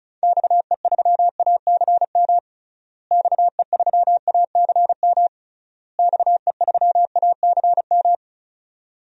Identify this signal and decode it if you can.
Morse code, heard as a keyed tone, about 36 wpm, 700 Hz. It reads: XE3ACM XE3ACM XE3ACM
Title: DXCC-100 Call Signs - 6 Characters - Repeated 3x - 35wpm